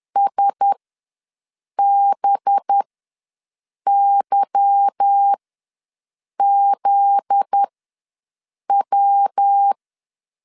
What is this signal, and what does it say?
Morse code, heard as a keyed tone, about 10 wpm, spaced as for 6 wpm, 800 Hz. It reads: SBYZW
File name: LEZIONI / 8 YJ45 / 8 YJ4 ws 10wpm 4020